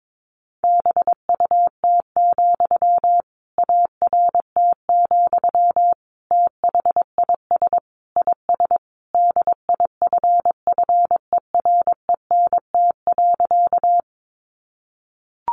Morse code, heard as a keyed tone, about 22 wpm, 700 Hz. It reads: BUT, ART, THIS IS DIFFERENT.
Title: Sentences from Top 600 Words 22wpm